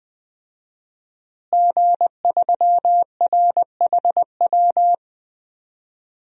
Morse code, heard as a keyed tone, about 20 wpm, 700 Hz. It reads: G3RHW
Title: DXCC-100 Call Signs - 5 Characters - Repeated 3x - 20wpm